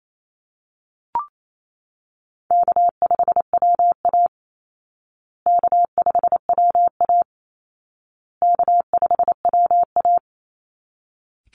Morse code, heard as a keyed tone, about 28 wpm, 700 Hz. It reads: K5WA K5WA K5WA
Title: Call Signs Repeated 2x - 28wpm